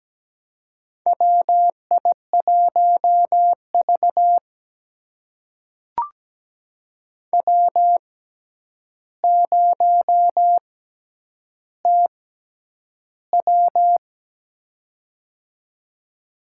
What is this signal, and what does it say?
Morse code, heard as a keyed tone, about 17 wpm, spaced as for 9 wpm, 700 Hz. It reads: WI1V W0TW
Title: USA Call Signs - 4 Characters - 6x Character Spacing - 17wpm